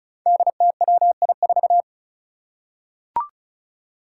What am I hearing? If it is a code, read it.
Morse code, heard as a keyed tone, about 36 wpm, 700 Hz. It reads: DTWI4
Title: Five Challenging Letter-Number Combinations 35wpm